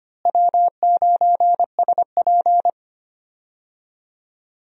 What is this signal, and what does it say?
Morse code, heard as a keyed tone, about 25 wpm, 700 Hz. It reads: W9SP
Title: USA Call Signs - 4 Characters - Repeated 3x - 25wpm